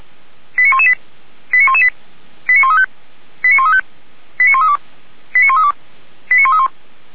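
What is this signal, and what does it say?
Five-tone selective call (ZVEI), heard as quick runs of five tones, tones of about 70 ms. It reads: 89118 89118 89135 89135 89132 89132 89131
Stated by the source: Received on SDR as ZVEI five-tone selective call 89103.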